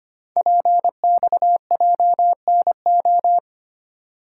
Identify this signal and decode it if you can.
Morse code, heard as a keyed tone, about 26 wpm, 700 Hz. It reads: PXJNO
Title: Code Groups - 5 Letters - 25wpm